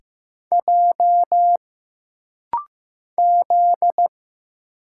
Morse code, heard as a keyed tone, about 15 wpm, 700 Hz. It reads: J Z
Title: Single Letters - Rapid-Fire 15wpm